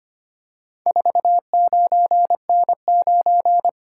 Morse code, heard as a keyed tone, about 25 wpm, 700 Hz. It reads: 49N9
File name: practice / 25wpm / Four Letter-Number Combinations - 25wpm